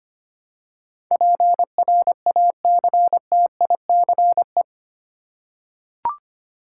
Morse code, heard as a keyed tone, about 25 wpm, 700 Hz. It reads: PRACTICE
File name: Eight Letter Words - 25wpm